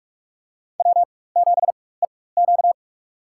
Morse code, heard as a keyed tone, about 42 wpm, spaced as for 20 wpm, 700 Hz. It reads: W7EQ
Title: USA Call Signs - 4 Characters - 4x Character Spacing - 45wpm